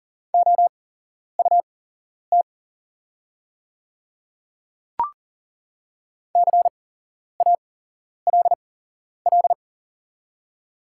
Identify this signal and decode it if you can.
Morse code, heard as a keyed tone, about 41 wpm, spaced as for 11 wpm, 690 Hz. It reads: OUT CALL
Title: 8x Character Spacing - Top 100 Words - 40wpm